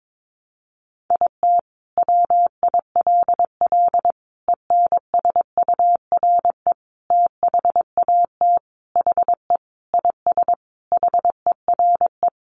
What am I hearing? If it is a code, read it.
Morse code, heard as a keyed tone, about 22 wpm, 700 Hz. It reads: IT WILL ENSURE THAT HE IS HERE
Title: Sentences from Top 1100 Words 22wpm